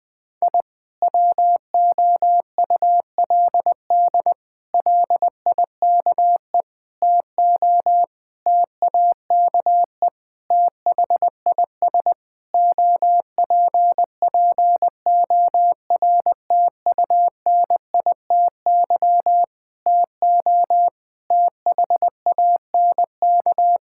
Morse code, heard as a keyed tone, about 21 wpm, 700 Hz. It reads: I WOULD LIKE TO TAKE THIS OPPORTUNITY TO THANK
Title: Sentences with 7-Word Phrases - Spoken Beginning - 20wpm